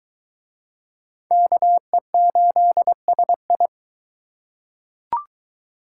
Morse code, heard as a keyed tone, about 22 wpm, 700 Hz. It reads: KE8SI